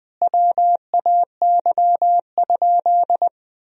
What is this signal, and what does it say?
Morse code, heard as a keyed tone, about 20 wpm, 700 Hz. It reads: WAY?
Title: 3x Word-Spacing - Sentences from Top 200 Words - 20wpm